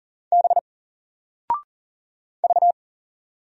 Morse code, heard as a keyed tone, about 40 wpm, 700 Hz. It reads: B V
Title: Single Letter-Number 40wpm